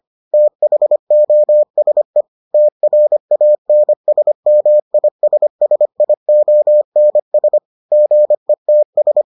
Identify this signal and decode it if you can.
Morse code, heard as a keyed tone, about 25 wpm, 600 Hz. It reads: THOSE TRANSMISSIONS GETS